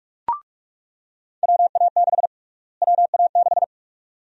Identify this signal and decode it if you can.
Morse code, heard as a keyed tone, about 46 wpm, 700 Hz. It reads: WA6 WA6
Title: USA Call Sign Prefixes - Repeated 2x - 45wpm